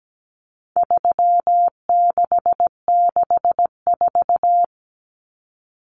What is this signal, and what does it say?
Morse code, heard as a keyed tone, about 17 wpm, 700 Hz. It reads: 3664
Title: Common Cut Numbers - 4-Digit 17wpm